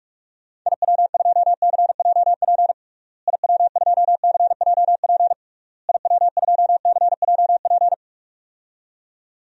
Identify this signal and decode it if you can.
Morse code, heard as a keyed tone, about 46 wpm, 700 Hz. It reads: IW2CJP IW2CJP IW2CJP
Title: DXCC-100 Call Signs - 6 Characters - Repeated 3x - 45wpm